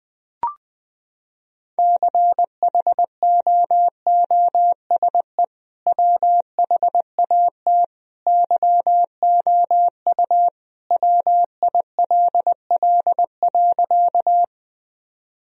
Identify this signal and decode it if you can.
Morse code, heard as a keyed tone, about 20 wpm, 700 Hz. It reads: CHOOSE WHAT YOU WILL.
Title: Sentences from Top 1000 Words - Review Part 4 - 20wpm